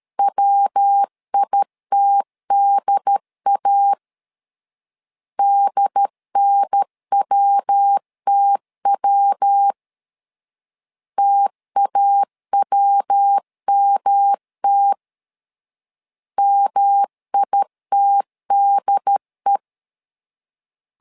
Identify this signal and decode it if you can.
Morse code, heard as a keyed tone, about 13 wpm, 800 Hz. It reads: WITDA DNWTW TAWMT MITDE
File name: LEZIONI / 3 MWRO / 3 MW 12wpm 1633